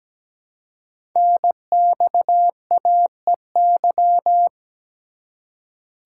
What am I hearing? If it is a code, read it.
Morse code, heard as a keyed tone, about 17 wpm, 700 Hz. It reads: NXAEY
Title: Code Groups - 5 Letters - 17wpm